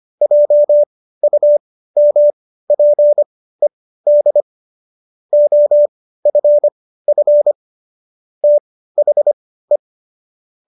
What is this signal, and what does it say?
Morse code, heard as a keyed tone, about 25 wpm, spaced as for 15 wpm, 590 Hz. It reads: JUMPED OFF THE